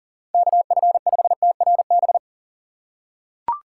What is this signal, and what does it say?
Morse code, heard as a keyed tone, about 40 wpm, 700 Hz. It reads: KF5TRB
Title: USA Call Signs - 6 Characters - 6x Character Spacing - 40wpm